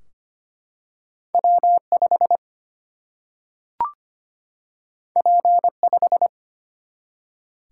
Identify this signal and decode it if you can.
Morse code, heard as a keyed tone, about 25 wpm, 700 Hz. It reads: W5 P5
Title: Single-Letter Single-Number 25wpm